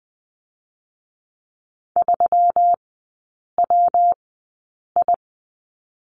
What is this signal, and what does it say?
Morse code, heard as a keyed tone, about 20 wpm, spaced as for 8 wpm, 700 Hz. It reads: 3WI
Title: 121 Lesson 20e - Number 3 - Three Characters 20wpm